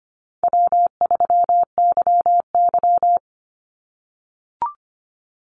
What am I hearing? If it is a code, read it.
Morse code, heard as a keyed tone, about 25 wpm, 700 Hz. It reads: W3YY
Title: Call Signs - 25wpm